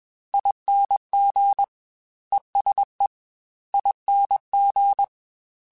Morse code, heard as a keyed tone, about 21 wpm, 800 Hz. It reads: ING ESE ING
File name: Special INGs-mp3s / Special INGs-mp3s / ESE ING @20